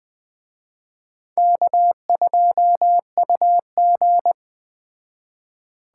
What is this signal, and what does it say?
Morse code, heard as a keyed tone, about 20 wpm, 700 Hz. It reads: K2UG
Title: Speed Racing - USA 1x2 Call Signs 17wpm